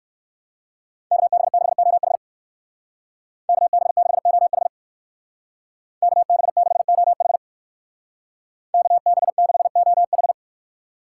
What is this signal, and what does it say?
Morse code, heard as a keyed tone, about 55 wpm, 700 Hz. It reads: KB6OH KB6OH KB6OH KB6OH
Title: Speed Racing - USA 2x2 Call Signs 45wpm